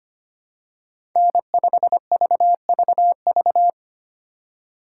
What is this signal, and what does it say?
Morse code, heard as a keyed tone, about 25 wpm, 700 Hz. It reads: N5VVV
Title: USA Call Signs - 5 Characters - 4x Character Spacing - 25wpm